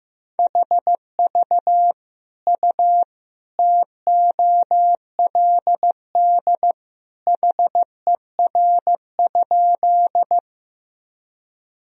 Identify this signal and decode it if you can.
Morse code, heard as a keyed tone, about 15 wpm, 700 Hz. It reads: HV U TOLD HER?
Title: Sentences from Top 400 Words and Abbreviations 15wpm